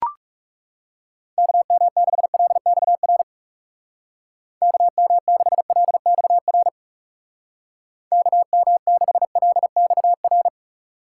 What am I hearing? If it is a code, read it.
Morse code, heard as a keyed tone, about 40 wpm, 700 Hz. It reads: KM6LXR KM6LXR KM6LXR